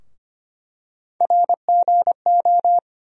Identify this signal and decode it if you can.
Morse code, heard as a keyed tone, about 25 wpm, 700 Hz. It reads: RGO
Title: USA Call Sign Postfixes - 3 Letters 25wpm